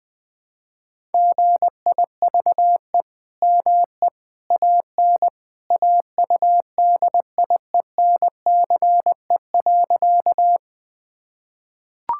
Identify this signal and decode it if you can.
Morse code, heard as a keyed tone, about 21 wpm, 700 Hz. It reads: GIVE ME AN AUDIENCE.
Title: Sentences from Top 1000 Words - Review Part 5 - 20wpm